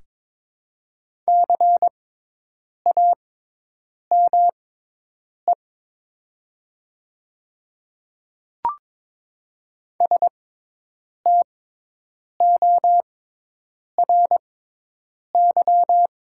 Morse code, heard as a keyed tone, about 22 wpm, spaced as for 8 wpm, 690 Hz. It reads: CAME STORY
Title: 6x Character Spacing - Top 300 Words - 22wpm